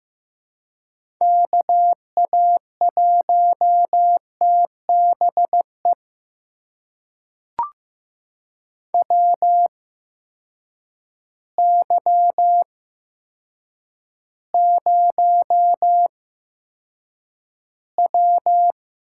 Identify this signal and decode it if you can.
Morse code, heard as a keyed tone, about 15 wpm, spaced as for 8 wpm, 700 Hz. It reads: KA1TBE WY0W
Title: USA Call Signs - 6 Characters - 8x Character Spacing - 15wpm